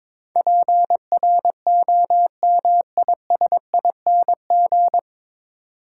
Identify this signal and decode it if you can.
Morse code, heard as a keyed tone, about 23 wpm, 700 Hz. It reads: PROMISING
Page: Morse Code Ninja practice 22wpm